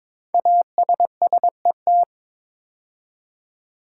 Morse code, heard as a keyed tone, about 22 wpm, 700 Hz. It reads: ASSET
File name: 5 Letter Words - First 3 Letters Given - 22wpm